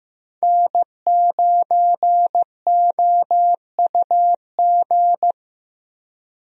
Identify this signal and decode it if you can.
Morse code, heard as a keyed tone, about 15 wpm, 700 Hz. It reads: N9OUG
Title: USA Call Signs - 5 Characters - Repeated 2x - 15wpm